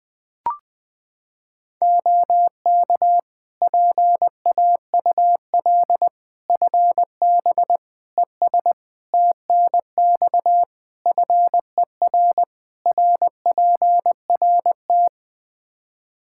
Morse code, heard as a keyed tone, about 20 wpm, 700 Hz. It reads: OK PAUL FB ES TNX FER RPRT
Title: Practice QSO - Exchange 2 - Info - 20wpm